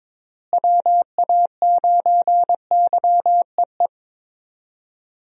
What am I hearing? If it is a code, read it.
Morse code, heard as a keyed tone, about 22 wpm, 700 Hz. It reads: WA9YEE
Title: USA Call Signs - 6 Characters - Repeated 3x - 22wpm